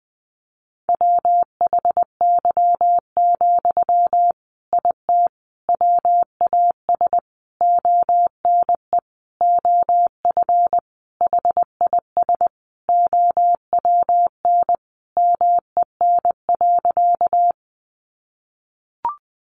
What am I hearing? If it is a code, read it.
Morse code, heard as a keyed tone, about 20 wpm, 700 Hz. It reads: WHY, IT WAS ONE OF HIS OWN MEN.
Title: Sentences from Top 300 Words 20wpm